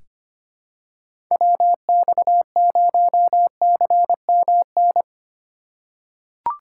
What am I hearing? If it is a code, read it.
Morse code, heard as a keyed tone, about 25 wpm, 700 Hz. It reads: WX0CMN
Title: USA Call Signs - 4x Character Spacing - 25wpm